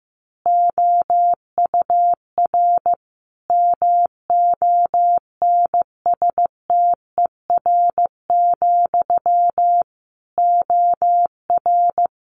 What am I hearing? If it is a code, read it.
Morse code, heard as a keyed tone, about 15 wpm, 700 Hz. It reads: OUR MONSTER, OR
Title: A Sound Of Thunder 15wpm